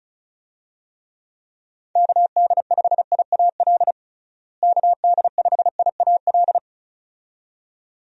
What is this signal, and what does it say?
Morse code, heard as a keyed tone, about 34 wpm, 700 Hz. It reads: KD5IAL KD5IAL
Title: USA Call Signs - 6 Characters - Repeated 2x - 35wpm